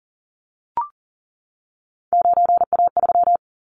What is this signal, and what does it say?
Morse code, heard as a keyed tone, about 40 wpm, 700 Hz. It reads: 9A3
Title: Callsign Prefixes for Top 100 DXCC - 40wpm